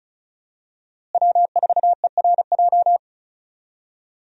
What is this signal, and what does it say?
Morse code, heard as a keyed tone, about 34 wpm, 700 Hz. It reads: W4ERJ
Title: USA Call Signs - 5 Characters - Repeated 3x - 35wpm